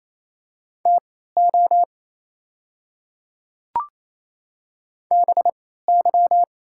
Morse code, heard as a keyed tone, about 28 wpm, spaced as for 16 wpm, 700 Hz. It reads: TO BY
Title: 3x Character Spacing - Top 100 Words - 28wpm